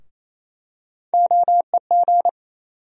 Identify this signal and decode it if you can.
Morse code, heard as a keyed tone, about 28 wpm, 700 Hz. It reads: OEG
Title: USA Call Sign Postfixes - 3 Letters 28wpm